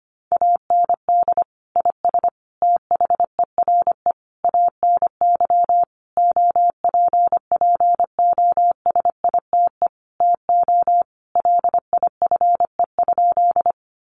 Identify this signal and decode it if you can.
Morse code, heard as a keyed tone, about 25 wpm, 700 Hz. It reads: AND IS THERE ANY OPPOSITE TO LIFE?
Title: Sentences from Top 1300 Words 25wpm